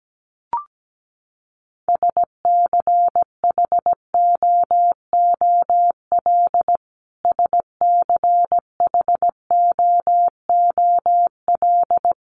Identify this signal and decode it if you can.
Morse code, heard as a keyed tone, about 17 wpm, 700 Hz. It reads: SCHOOL SCHOOL
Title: Top 500 Words Repeated 17wpm